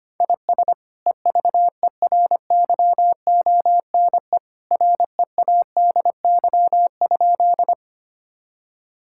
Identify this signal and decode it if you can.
Morse code, heard as a keyed tone, about 25 wpm, 700 Hz. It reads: IS EVERYONE READY?
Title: Sentences from Top 1000 Words - Review Part 5 - 25wpm